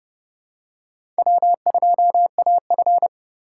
Morse code, heard as a keyed tone, about 31 wpm, 700 Hz. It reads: W2AF
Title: USA Call Signs - 4 Characters - 3x Character Spacing - 30wpm